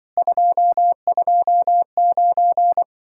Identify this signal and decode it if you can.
Morse code, heard as a keyed tone, about 24 wpm, 700 Hz. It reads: 229
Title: Speed-Racing Common QSO Phrases Encore - 20wpm